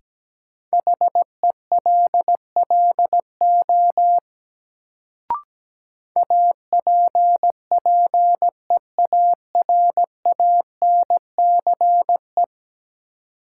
Morse code, heard as a keyed tone, about 17 wpm, 700 Hz. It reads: HELLO APPEARANCE
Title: Sentences from Top 1100 Words - New Word Drill - 17wpm